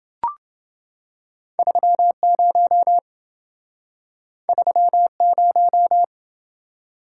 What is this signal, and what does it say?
Morse code, heard as a keyed tone, about 28 wpm, 700 Hz. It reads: 30 30